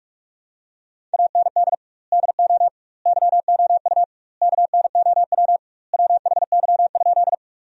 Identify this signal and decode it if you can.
Morse code, heard as a keyed tone, about 41 wpm, 700 Hz. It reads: AND DO YOU KNOW WHY?
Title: 2x Word-Spacing - Sentences from Top 200 Words - 45wpm